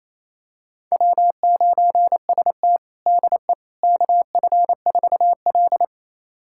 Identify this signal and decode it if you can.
Morse code, heard as a keyed tone, about 28 wpm, 700 Hz. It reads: W9ST DE KF4L
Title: USA Callsign DE Callsign - 4 Characters - 28wpm